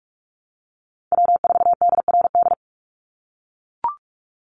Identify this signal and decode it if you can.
Morse code, heard as a keyed tone, about 44 wpm, 700 Hz. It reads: W4DRD
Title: USA Call Signs 45wpm